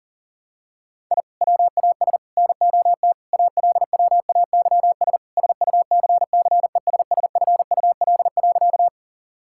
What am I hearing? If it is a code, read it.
Morse code, heard as a keyed tone, about 40 wpm, 700 Hz. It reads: I WAS NOT ALWAYS SUCCESSFUL.
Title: Sentences from Top 1000 Words - Review Part 6 - 40wpm